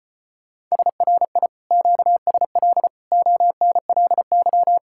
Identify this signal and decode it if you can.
Morse code, heard as a keyed tone, about 30 wpm, 700 Hz. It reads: SRI QSL ONLY
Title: Speed-Racing Common QSO Phrases - 25wpm